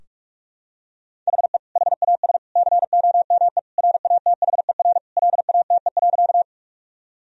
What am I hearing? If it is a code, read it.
Morse code, heard as a keyed tone, about 46 wpm, 700 Hz. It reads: HE HAS COME RATHER LATE.
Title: Sentences from Top 1000 Words - Review Part 4 - 45wpm